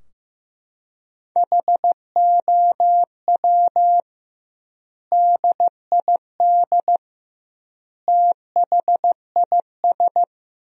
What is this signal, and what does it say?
Morse code, heard as a keyed tone, about 15 wpm, 700 Hz. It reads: HOW DID THIS
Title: 2x Word-Spacing - Sentences from Top 400 Words - 15wpm